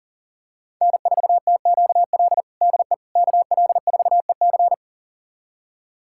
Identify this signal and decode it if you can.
Morse code, heard as a keyed tone, about 40 wpm, 700 Hz. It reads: N4TQL DE KL4EC
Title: USA Callsign DE Callsign - 5 Characters - 40wpm